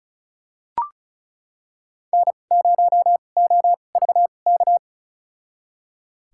Morse code, heard as a keyed tone, about 36 wpm, spaced as for 25 wpm, 700 Hz. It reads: N0OVK
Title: USA Call Signs - 5 Characters - 2x Character Spacing - 35wpm